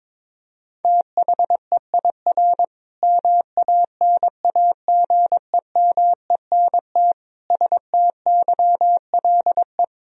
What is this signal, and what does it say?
Morse code, heard as a keyed tone, about 22 wpm, 700 Hz. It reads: THEIR MANAGEMENT STYLE
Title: Sentences from Top 1000 Words - Review Part 4 - 22wpm